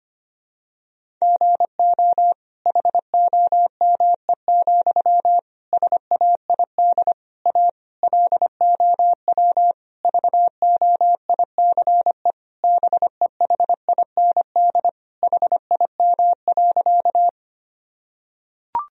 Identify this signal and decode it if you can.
Morse code, heard as a keyed tone, about 25 wpm, 700 Hz. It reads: GO HOME, SAID A LOW VOICE BEHIND HIM.
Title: Sentences from Top 1000 Words - Review Part 1 - 25wpm